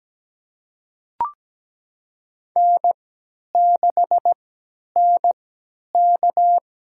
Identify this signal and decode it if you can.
Morse code, heard as a keyed tone, about 17 wpm, spaced as for 10 wpm, 700 Hz. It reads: N6NK
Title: USA Call Signs - 4 Characters - 3x Character Spacing - 17wpm